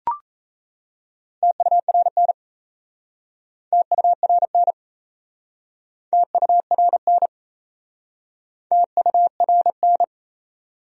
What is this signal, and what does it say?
Morse code, heard as a keyed tone, about 37 wpm, 700 Hz. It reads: TURN TURN TURN TURN